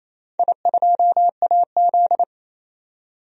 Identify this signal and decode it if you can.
Morse code, heard as a keyed tone, about 28 wpm, 700 Hz. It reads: I2AZ
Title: DXCC-100 Call Signs - 4 Characters - 28wpm